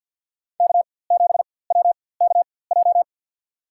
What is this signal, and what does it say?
Morse code, heard as a keyed tone, about 53 wpm, spaced as for 24 wpm, 700 Hz. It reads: K7WKJ
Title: USA Call Signs - 4x Character Spacing - 50wpm